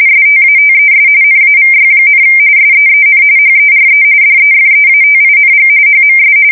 RTTY, about 45.45 baud. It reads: STATES NAVY, SEEN HERE DEFEATING HMS
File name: fsk170 / usnavy1